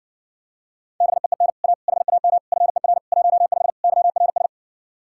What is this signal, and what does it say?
Morse code, heard as a keyed tone, about 60 wpm, 700 Hz. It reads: BEEN A HAM FER 15 YRS